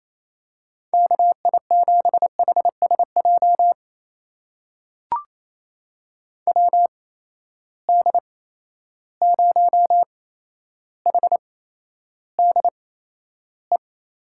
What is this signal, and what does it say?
Morse code, heard as a keyed tone, about 27 wpm, spaced as for 12 wpm, 700 Hz. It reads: KI7HSJ WD0HDE
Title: USA Call Signs - 6 Characters - 8x Character Spacing - 28wpm